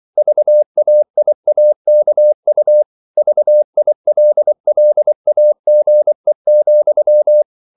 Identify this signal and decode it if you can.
Morse code, heard as a keyed tone, about 24 wpm, 600 Hz. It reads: VAIAKU VILLAGE,
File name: CW / Practice / NationalCapitals24